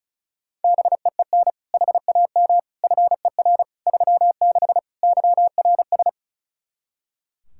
Morse code, heard as a keyed tone, about 34 wpm, 700 Hz. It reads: BEEN HAM FER 36 YRS
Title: Practice QSO - Exchange 3 - More Info - 35wpm